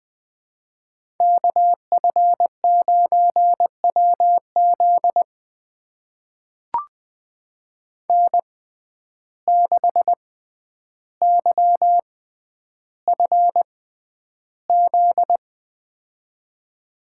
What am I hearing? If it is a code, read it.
Morse code, heard as a keyed tone, about 20 wpm, spaced as for 10 wpm, 700 Hz. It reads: KF9WZ N6YFZ
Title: USA Call Signs - 5 Characters - 6x Character Spacing - 20wpm